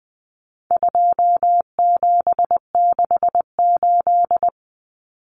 Morse code, heard as a keyed tone, about 20 wpm, 700 Hz. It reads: 2768
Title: Common Cut Numbers - 4-Digit 20wpm